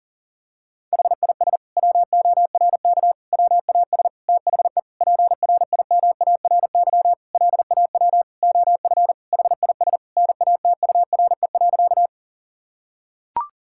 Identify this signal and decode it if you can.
Morse code, heard as a keyed tone, about 40 wpm, 700 Hz. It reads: HIS WORK WAS THE PRIMARY LAW OF HIS NATURE.
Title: Sentences from Top 1000 Words - Review Part 5 - 40wpm